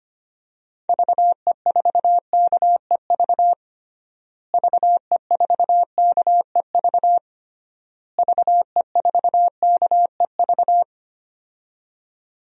Morse code, heard as a keyed tone, about 25 wpm, 700 Hz. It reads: VE4KEV VE4KEV VE4KEV
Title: DXCC-100 Call Signs - 6 Characters - Repeated 3x - 25wpm